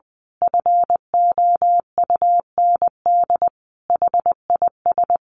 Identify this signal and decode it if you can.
Morse code, heard as a keyed tone, about 20 wpm, 700 Hz. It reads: FOUND HIS
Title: A Sound Of Thunder 20wpm